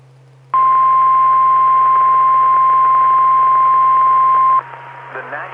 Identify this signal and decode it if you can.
Emergency Alert System (SAME) tones (the 1050 Hz attention tone), with speech. It is heard from 0.5 to 4.6 s.